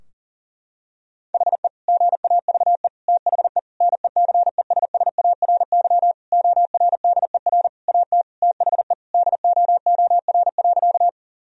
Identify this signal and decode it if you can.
Morse code, heard as a keyed tone, about 40 wpm, 700 Hz. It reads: HE GAVE THE NECESSARY ORDER AT THE DOOR.
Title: Sentences from Top 1000 Words - Review Part 2 - 40wpm